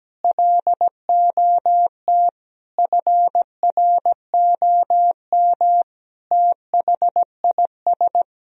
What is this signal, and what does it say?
Morse code, heard as a keyed tone, about 17 wpm, 700 Hz. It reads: LOT FROM THIS
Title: Sentences from Top 1000 Words - Review Part 2 - 17wpm